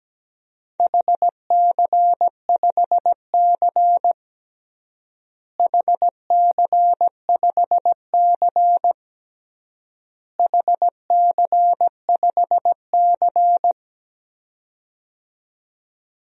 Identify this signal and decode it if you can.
Morse code, heard as a keyed tone, about 17 wpm, 700 Hz. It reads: HC5C HC5C HC5C
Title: DXCC-100 Call Signs - 4 Characters - Repeated 3x - 17wpm